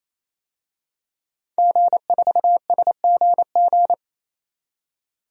Morse code, heard as a keyed tone, about 28 wpm, 700 Hz. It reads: G4SGG
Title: DXCC-100 Call Signs - 5 Characters - 28wpm